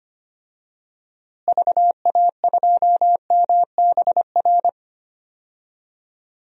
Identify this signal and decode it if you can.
Morse code, heard as a keyed tone, about 25 wpm, 700 Hz. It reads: VA2MBR
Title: DXCC-100 Call Signs - 6 Characters - Repeated 3x - 25wpm